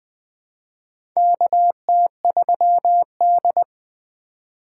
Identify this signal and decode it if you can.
Morse code, heard as a keyed tone, about 20 wpm, 700 Hz. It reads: KT3D
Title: USA Call Signs - 4x Character Spacing - 20wpm